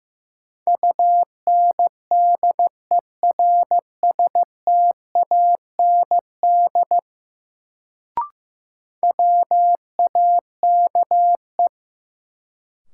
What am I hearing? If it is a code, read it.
Morse code, heard as a keyed tone, about 15 wpm, 700 Hz. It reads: UNDERSTAND WAKE
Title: Sets of 4 Words - New Word Drill - 15wpm